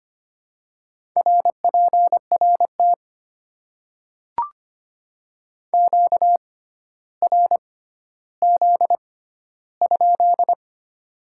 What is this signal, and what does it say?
Morse code, heard as a keyed tone, about 25 wpm, spaced as for 13 wpm, 700 Hz. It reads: RPRT QRZ?
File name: Top 90 QSO Elements - 6x Character Spacing - 25wpm